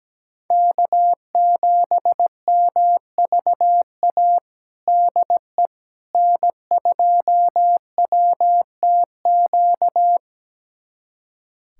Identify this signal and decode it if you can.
Morse code, heard as a keyed tone, about 17 wpm, 700 Hz. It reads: K7MVA DE N2WTQ
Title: USA Callsign DE Callsign - 5 Characters - 17wpm